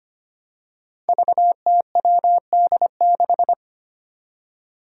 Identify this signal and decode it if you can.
Morse code, heard as a keyed tone, about 25 wpm, 700 Hz. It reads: VTWD6
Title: Five Challenging Letter-Number Combinations 25wpm